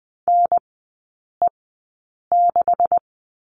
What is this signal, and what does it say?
Morse code, heard as a keyed tone, about 20 wpm, spaced as for 8 wpm, 700 Hz. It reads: NE6
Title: 178 Lesson 29e - Slash - Three Characters 20wpm